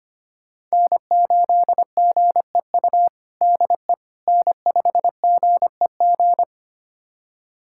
Morse code, heard as a keyed tone, about 25 wpm, 700 Hz. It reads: N8GEU DE N5GEG
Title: USA Callsign DE Callsign - 5 Characters - 25wpm